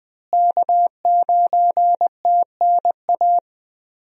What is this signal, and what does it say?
Morse code, heard as a keyed tone, about 20 wpm, 700 Hz. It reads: K9TNA
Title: USA Call Signs - 5 Characters - Repeated 3x - 20wpm